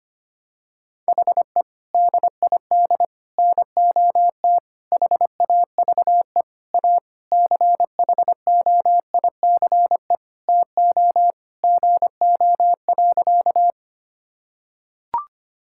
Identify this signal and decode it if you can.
Morse code, heard as a keyed tone, about 25 wpm, 700 Hz. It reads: HE DID NOT HAVE A CHOICE TO GO.
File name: practice / 25wpm / Sentences with 4-Word Phrases - 25wpm